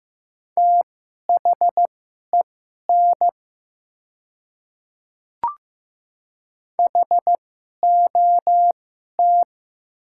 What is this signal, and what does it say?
Morse code, heard as a keyed tone, about 15 wpm, spaced as for 11 wpm, 700 Hz. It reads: THEN HOT
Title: 2x Character Spacing - Top 300 Words - 15wpm